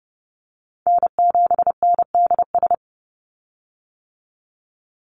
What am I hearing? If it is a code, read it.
Morse code, heard as a keyed tone, about 30 wpm, 700 Hz. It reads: N7NDS
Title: USA Call Signs 30wpm